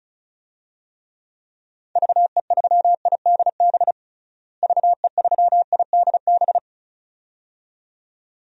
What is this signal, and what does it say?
Morse code, heard as a keyed tone, about 34 wpm, 700 Hz. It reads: VE3IDB VE3IDB
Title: DXCC-100 Call Signs - 6 Characters - Repeated 2x - 35wpm